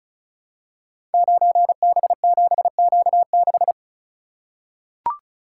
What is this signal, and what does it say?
Morse code, heard as a keyed tone, about 36 wpm, 700 Hz. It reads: 9B7Q6
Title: Code Groups - 5 Letter:Numbers - 35wpm